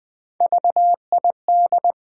Morse code, heard as a keyed tone, about 20 wpm, 700 Hz. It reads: VID
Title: Build A Word - 1 Letter At A Time - 3 to 5-Letter Words - 20wpm